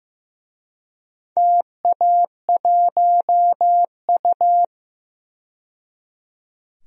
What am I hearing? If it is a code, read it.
Morse code, heard as a keyed tone, about 15 wpm, 700 Hz. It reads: TA1U
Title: DXCC-100 Call Signs - 4 Characters - 15wpm